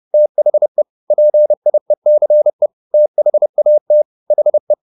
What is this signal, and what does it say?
Morse code, heard as a keyed tone, about 30 wpm, 600 Hz. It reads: THE PIECE THAT HE
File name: the old man and the sea / the old man and the sea 1 30wpm